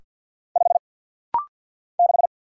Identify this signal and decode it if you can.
Morse code, heard as a keyed tone, about 48 wpm, 700 Hz. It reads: 5 6
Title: Single Letter-Number - Rapid-Fire 50wpm